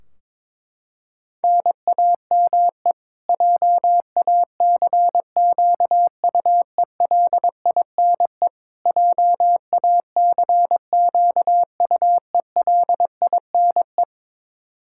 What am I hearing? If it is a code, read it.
Morse code, heard as a keyed tone, about 22 wpm, 700 Hz. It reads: NAME JACQUELINE JACQUELINE
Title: K1USN SST QSO Practice 22wpm